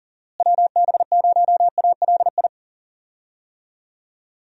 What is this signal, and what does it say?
Morse code, heard as a keyed tone, about 37 wpm, 700 Hz. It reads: WB0ALI
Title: USA Call Signs - 6 Characters - Repeated 2x - 40wpm